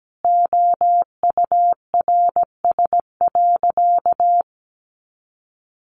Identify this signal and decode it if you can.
Morse code, heard as a keyed tone, about 17 wpm, 700 Hz. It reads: OURS.
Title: Sentences from Top 1300 Words 17wpm